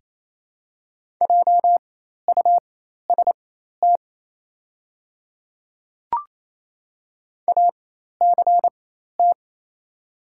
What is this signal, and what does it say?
Morse code, heard as a keyed tone, about 27 wpm, spaced as for 13 wpm, 700 Hz. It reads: JUST ACT
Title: 4x Character Spacing - Top 200 Words - 28wpm